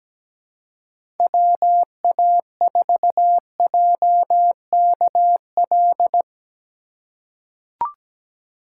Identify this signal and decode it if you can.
Morse code, heard as a keyed tone, about 17 wpm, 700 Hz. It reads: WA4JKL
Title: K1USN SST QSO Practice - 17wpm